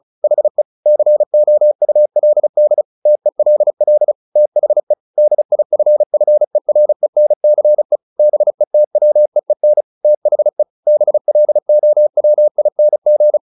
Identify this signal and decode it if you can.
Morse code, heard as a keyed tone, about 34 wpm, 600 Hz. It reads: HE COULD TELL THE DIFFERENCE BETWEEN THE BLOWING
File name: the old man and the sea / the old man and the sea 1 35wpm